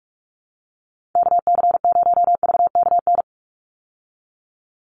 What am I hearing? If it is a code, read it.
Morse code, heard as a keyed tone, about 44 wpm, 700 Hz. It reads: KC0VKN
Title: Call Signs Repeated 2x - 45wpm